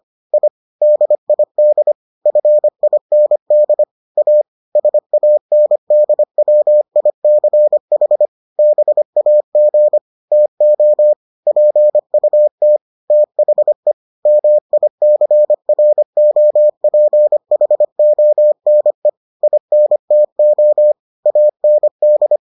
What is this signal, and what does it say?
Morse code, heard as a keyed tone, about 25 wpm, 600 Hz. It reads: I DID FIND A SANDWICH BAG TO PUT THE MICROPHONE INTO AND